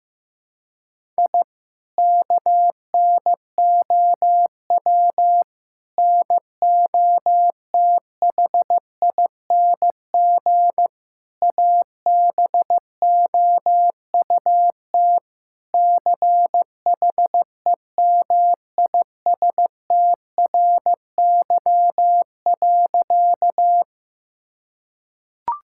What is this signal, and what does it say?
Morse code, heard as a keyed tone, about 15 wpm, 700 Hz. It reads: I KNOW NOTHING ABOUT CHEMISTRY.
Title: Sentences from Top 1000 Words - Review Part 6 - 15wpm